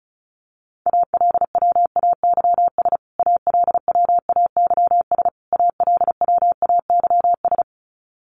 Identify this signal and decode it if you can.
Morse code, heard as a keyed tone, about 34 wpm, 700 Hz. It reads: ALWAYS ALWAYS ALWAYS
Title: Top 500 Words Repeated 35wpm